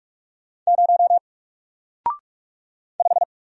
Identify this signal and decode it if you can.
Morse code, heard as a keyed tone, about 44 wpm, 700 Hz. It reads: 0 5
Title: Single Character (Letter:Number:Punctuation:Prosign) Alt 45wpm